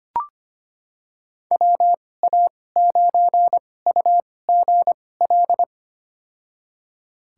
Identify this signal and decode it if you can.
Morse code, heard as a keyed tone, about 25 wpm, spaced as for 18 wpm, 700 Hz. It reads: WA9UGL
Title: USA Call Signs - 6 Characters - 2x Character Spacing - 25wpm